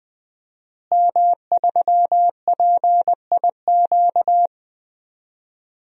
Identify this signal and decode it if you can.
Morse code, heard as a keyed tone, about 20 wpm, 700 Hz. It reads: M3PIQ
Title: DXCC-100 Call Signs - 5 Characters - Repeated 2x - 20wpm